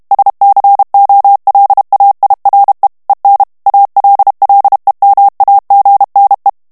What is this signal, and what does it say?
Morse code, heard as a keyed tone, about 32 wpm, 800 Hz. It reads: SCOLAIRE EN ALLEMAGNE
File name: morse mp3 / FRANCAIS TEXTE CLAIR 1 VITESSE 30 MOTS MINUTES